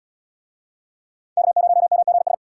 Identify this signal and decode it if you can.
Morse code, heard as a keyed tone, about 77 wpm, 700 Hz. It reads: K0MGS